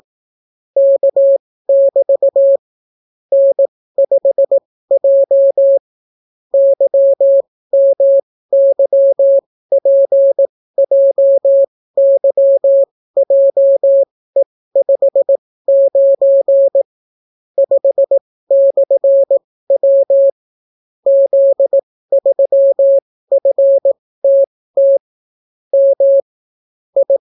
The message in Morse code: K<BT> N5J YMYPJYJE59 5/W Z3FTT M I

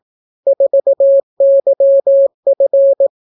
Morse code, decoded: 4YF

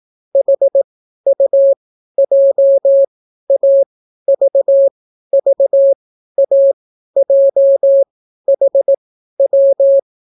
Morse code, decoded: HUJAVVAJHW

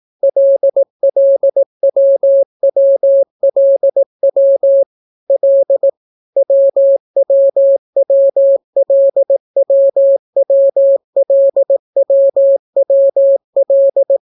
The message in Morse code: LLWWLW L WWWLWWLWWL

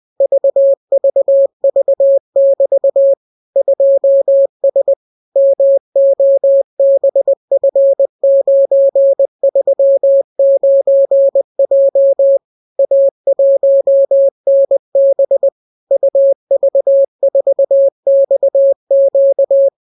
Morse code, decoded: VVV<BT> 2S MOBF939J A1NB UV4XQ

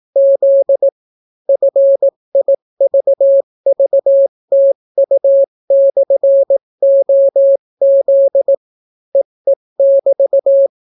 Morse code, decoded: Z FIVVTU/OZ EE<BT>

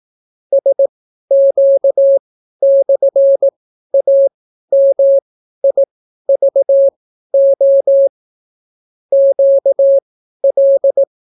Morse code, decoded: SQ/AMIVO QL

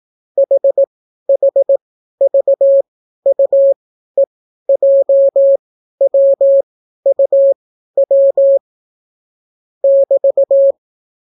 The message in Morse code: HHVUEJWUW <BT>

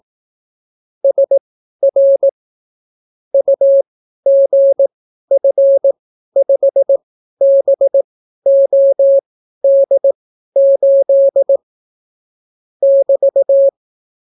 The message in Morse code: SR UGF5BOD8 <BT>

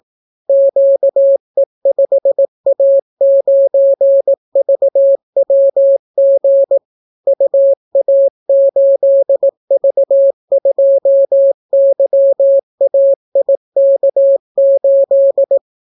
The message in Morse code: QE5A9VWG UA8V2YAIK8

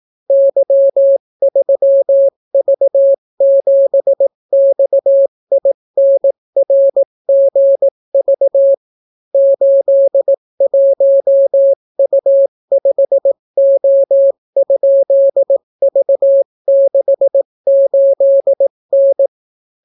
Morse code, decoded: Y3V7XINRGV 81U5O?V68N

